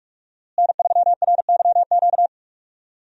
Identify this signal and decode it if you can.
Morse code, heard as a keyed tone, about 46 wpm, 700 Hz. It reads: N3RYQ
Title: USA Call Signs - 5 Characters - Repeated 2x - 45wpm